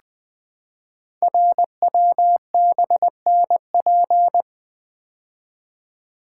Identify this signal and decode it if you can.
Morse code, heard as a keyed tone, about 20 wpm, 700 Hz. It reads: RWBNP
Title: Code Groups - 5 Letters - 20wpm